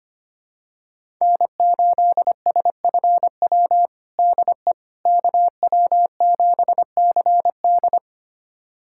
Morse code, heard as a keyed tone, about 25 wpm, 700 Hz. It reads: N8SFW DE KW7CD
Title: USA Callsign DE Callsign - 5 Characters - 25wpm